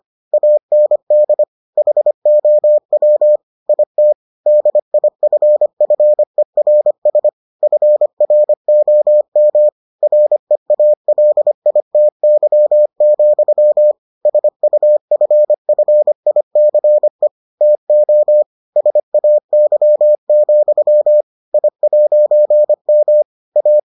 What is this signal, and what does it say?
Morse code, heard as a keyed tone, about 25 wpm, 600 Hz. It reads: AND HOW IT DIFFERS FROM REALITY, SUFFICE TO SAY, I'M A